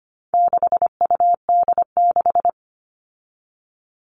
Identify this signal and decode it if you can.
Morse code, heard as a keyed tone, about 25 wpm, 700 Hz. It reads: 6UD6
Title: All Cut Numbers - 4-Digit 25wpm